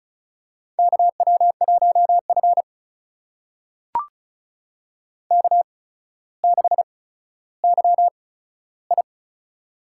Morse code, heard as a keyed tone, about 36 wpm, spaced as for 15 wpm, 700 Hz. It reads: KW1F K6YI
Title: USA Call Signs - 4 Characters - 8x Character Spacing - 35wpm